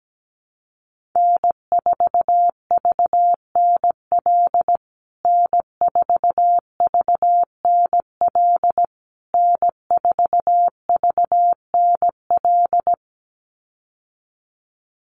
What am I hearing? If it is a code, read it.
Morse code, heard as a keyed tone, about 17 wpm, 700 Hz. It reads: N4VNL N4VNL N4VNL
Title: USA Call Signs Repeated 3x 17wpm